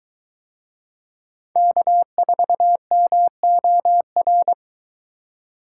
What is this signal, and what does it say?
Morse code, heard as a keyed tone, about 22 wpm, 700 Hz. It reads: K4MOR